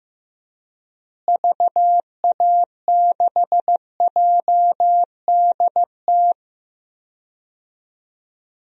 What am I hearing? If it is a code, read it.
Morse code, heard as a keyed tone, about 15 wpm, 700 Hz. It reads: VA6JDT
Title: DXCC-100 Call Signs - 6 Characters - Repeated 3x - 15wpm